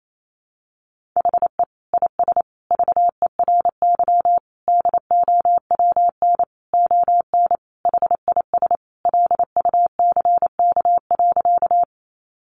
Morse code, heard as a keyed tone, about 28 wpm, 700 Hz. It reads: HE IS VERY DOWN ON HIS LUCK.
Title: Sentences from Top 1500 Words 28wpm